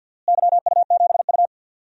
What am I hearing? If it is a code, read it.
Morse code, heard as a keyed tone, about 47 wpm, 700 Hz. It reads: YU7U
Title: DXCC-100 Call Signs - 4 Characters - Repeated 3x - 50wpm